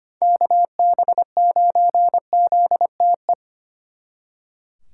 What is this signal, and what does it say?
Morse code, heard as a keyed tone, about 30 wpm, 700 Hz. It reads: KB9ZTE
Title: Speed Racing - USA Call Signs 25wpm